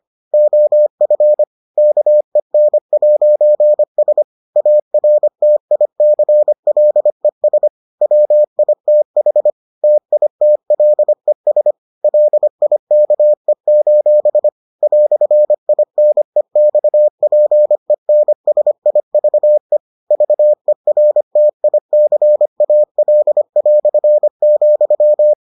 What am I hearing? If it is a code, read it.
Morse code, heard as a keyed tone, about 25 wpm, 600 Hz. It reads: OF KEN'S ARTICLES WITH TITLES LIKE: "INEXPENSIVE VERTICAL",